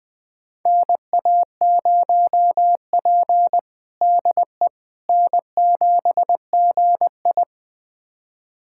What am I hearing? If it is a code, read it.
Morse code, heard as a keyed tone, about 21 wpm, 700 Hz. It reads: NA0P DE N7GI
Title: USA Callsign DE Callsign - 4 Characters - 20wpm